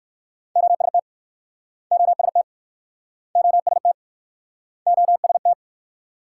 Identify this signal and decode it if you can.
Morse code, heard as a keyed tone, about 64 wpm, 700 Hz. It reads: OST OST OST OST